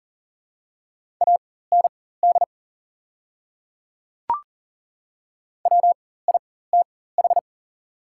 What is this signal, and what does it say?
Morse code, heard as a keyed tone, about 40 wpm, spaced as for 19 wpm, 700 Hz. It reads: AND WITH
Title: 4x Character Spacing - Top 200 Words - 40wpm